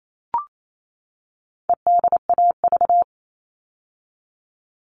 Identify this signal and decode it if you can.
Morse code, heard as a keyed tone, about 28 wpm, 700 Hz. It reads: EDAV
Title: All Cut Numbers - 4-Digit 28wpm